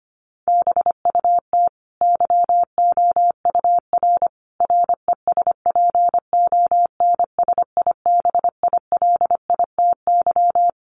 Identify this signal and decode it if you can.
Morse code, heard as a keyed tone, about 25 wpm, 700 Hz. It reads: BUT YOUR RESPONSIBILITY
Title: Sentences from Top 1200 Words 25wpm